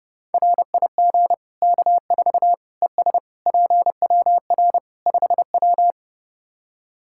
Morse code, heard as a keyed tone, about 31 wpm, 700 Hz. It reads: RIG K4 ES PWR 5W
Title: K1USN SST QSO Practice 30wpm